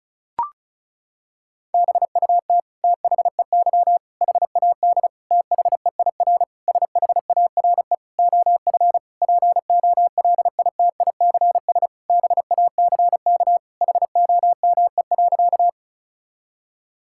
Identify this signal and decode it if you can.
Morse code, heard as a keyed tone, about 36 wpm, 700 Hz. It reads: BUT THEY HAD THEIR SHARE OF POLITICS BACK HOME.
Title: Sentences from Top 1000 Words - Review Part 6 - 35wpm